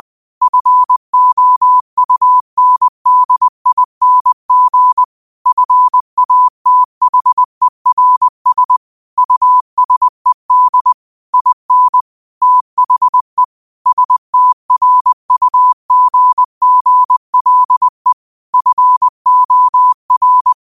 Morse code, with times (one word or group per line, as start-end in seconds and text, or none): 0.4-5.0 FOUNDING
5.5-8.8 FATHERS
9.2-10.9 USED
11.3-12.0 IN
12.4-13.4 THE
13.9-18.1 STRUGGLE
18.5-20.5 FOR